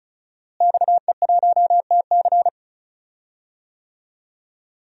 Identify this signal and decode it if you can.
Morse code, heard as a keyed tone, about 36 wpm, 700 Hz. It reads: XE1TC
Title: DXCC-100 Call Signs - 5 Characters - Repeated 3x - 35wpm